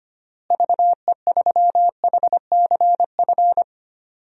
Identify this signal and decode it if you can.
Morse code, heard as a keyed tone, about 25 wpm, 700 Hz. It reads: VE3HCF
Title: DXCC-100 Call Signs - 6 Characters - Repeated 3x - 25wpm